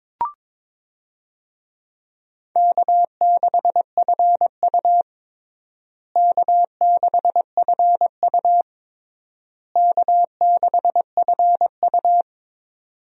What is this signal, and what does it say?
Morse code, heard as a keyed tone, about 22 wpm, 700 Hz. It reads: K6FU K6FU K6FU
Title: USA Call Signs - 4 Characters - Repeated 3x - 22wpm